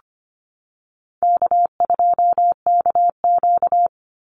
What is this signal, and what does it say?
Morse code, heard as a keyed tone, about 25 wpm, 700 Hz. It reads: K2KQ
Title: Easy Call Signs - 25wpm